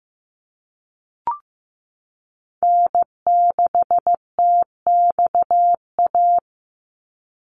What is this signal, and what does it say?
Morse code, heard as a keyed tone, about 15 wpm, 700 Hz. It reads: N6TXA
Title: USA Call Signs 15wpm